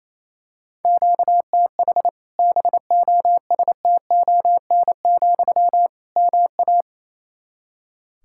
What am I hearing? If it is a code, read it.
Morse code, heard as a keyed tone, about 28 wpm, 700 Hz. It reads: QTH BOSTON, MA
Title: Practice QSO - Exchange 1 - Beginning - 28wpm